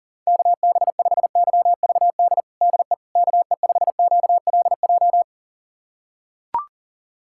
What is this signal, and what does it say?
Morse code, heard as a keyed tone, about 40 wpm, 700 Hz. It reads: KB5YVD DE KE5QLJ
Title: USA Callsign DE Callsign - 6 Characters - 40wpm